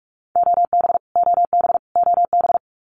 Morse code, heard as a keyed tone, about 44 wpm, 700 Hz. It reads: OB OB OB
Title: Abbreviations Repeated 3x - 45wpm